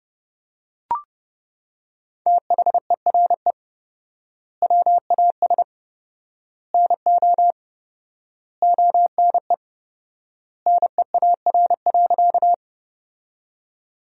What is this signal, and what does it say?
Morse code, heard as a keyed tone, about 30 wpm, 700 Hz. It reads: THERE WAS NO ONE NEAR.
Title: 4x Word-Spacing - Sentences from Top 200 Words - 30wpm